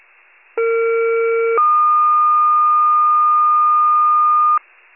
Two-tone page, heard as a one-second tone, then a three-second tone, about 460 then 1220 Hz. It is heard from 0.6 to 4.6 s.